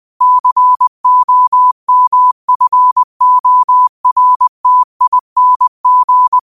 Morse code, heard as a keyed tone, about 20 wpm, 1000 Hz. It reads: COMFORTING
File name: qotdfiles / 20wpm